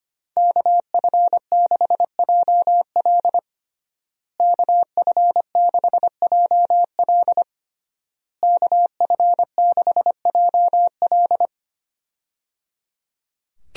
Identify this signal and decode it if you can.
Morse code, heard as a keyed tone, about 25 wpm, 700 Hz. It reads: KF6JL KF6JL KF6JL
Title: USA Call Signs - 5 Characters - Repeated 3x - 25wpm